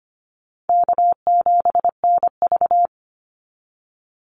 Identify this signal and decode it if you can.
Morse code, heard as a keyed tone, about 25 wpm, 700 Hz. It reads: K7NV
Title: Easy Call Signs Repeated 3x - 25wpm